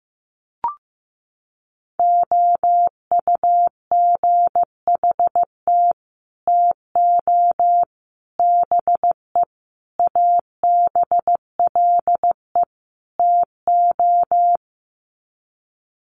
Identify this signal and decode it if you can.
Morse code, heard as a keyed tone, about 15 wpm, 700 Hz. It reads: OUGHT TO BE ABLE TO
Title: Sets of 5 Words Encore2 15wpm